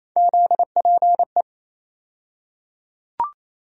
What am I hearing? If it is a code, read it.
Morse code, heard as a keyed tone, about 28 wpm, 700 Hz. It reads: ZPE
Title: USA Call Sign Postfixes - 3 Letters 28wpm